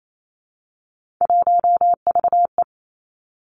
Morse code, heard as a keyed tone, about 28 wpm, 700 Hz. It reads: 1VE
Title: Three Challenging Letter-Number Combinations 28wpm